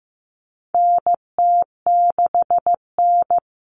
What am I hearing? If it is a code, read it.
Morse code, heard as a keyed tone, about 15 wpm, 700 Hz. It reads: NT6N
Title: All Cut Numbers - 4-Digit 15wpm